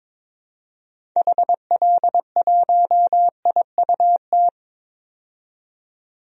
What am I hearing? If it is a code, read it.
Morse code, heard as a keyed tone, about 22 wpm, 700 Hz. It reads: HL1IUT
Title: DXCC-100 Call Signs - 6 Characters - 22wpm